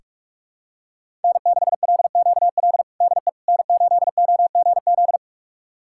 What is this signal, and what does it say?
Morse code, heard as a keyed tone, about 45 wpm, 700 Hz. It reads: N6LQL DE N8OGZ
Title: USA Callsign DE Callsign - 5 Characters - 45wpm